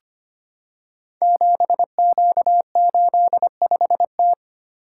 Morse code, heard as a keyed tone, about 25 wpm, 700 Hz. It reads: 7Q85T
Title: Code Groups - 5 Letter:Numbers - 25wpm